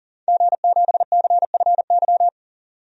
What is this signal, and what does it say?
Morse code, heard as a keyed tone, about 39 wpm, 700 Hz. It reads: G7CFY
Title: DXCC-100 Call Signs - 5 Characters - 40wpm